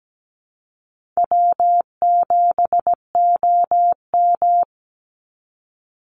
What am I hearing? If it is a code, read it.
Morse code, heard as a keyed tone, about 17 wpm, 700 Hz. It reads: W7OM
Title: Easy Call Signs Repeated 2x - 17wpm